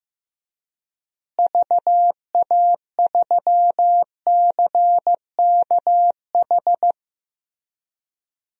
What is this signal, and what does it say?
Morse code, heard as a keyed tone, about 15 wpm, 700 Hz. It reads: VA3CKH
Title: DXCC-100 Call Signs - 6 Characters - Repeated 3x - 15wpm